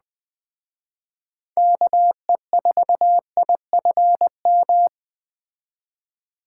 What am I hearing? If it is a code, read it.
Morse code, heard as a keyed tone, about 20 wpm, 700 Hz. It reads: KE4IFM
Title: Speed Racing - USA 2x3 Call Signs 17wpm